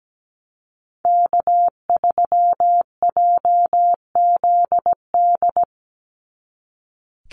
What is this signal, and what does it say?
Morse code, heard as a keyed tone, about 17 wpm, 700 Hz. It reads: K3JZD
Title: NA SOTA Chaser Callsigns 17wpm